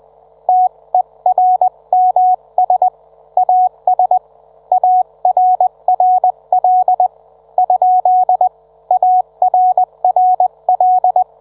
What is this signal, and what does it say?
Morse code, heard as a keyed tone, about 20 wpm, 740 Hz. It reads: TERMS AS ARRL ? ARRL